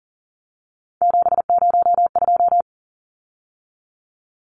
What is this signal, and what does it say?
Morse code, heard as a keyed tone, about 40 wpm, 700 Hz. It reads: 702
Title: Numbers - 3-Digit 40wpm